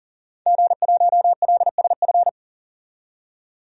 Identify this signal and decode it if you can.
Morse code, heard as a keyed tone, about 37 wpm, 700 Hz. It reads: G1LSF
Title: DXCC-100 Call Signs - 5 Characters - 40wpm